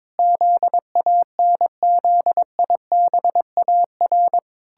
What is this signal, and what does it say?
Morse code, heard as a keyed tone, about 22 wpm, 700 Hz. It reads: ZANZIBAR
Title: Pangram Challenge v2 - 22wpm